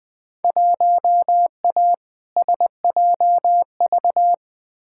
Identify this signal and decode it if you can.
Morse code, heard as a keyed tone, about 20 wpm, 700 Hz. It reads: 1A SJV
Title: Field Day Exchanges Encore - 20wpm